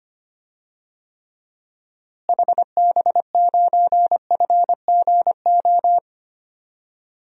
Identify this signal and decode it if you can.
Morse code, heard as a keyed tone, about 25 wpm, 700 Hz. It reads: HB9FGO
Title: DXCC-100 Call Signs - 6 Characters - Repeated 2x - 25wpm